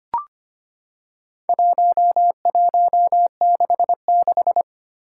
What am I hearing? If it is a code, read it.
Morse code, heard as a keyed tone, about 25 wpm, 700 Hz. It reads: 1166
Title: Four Challenging Letter-Number Combinations 25wpm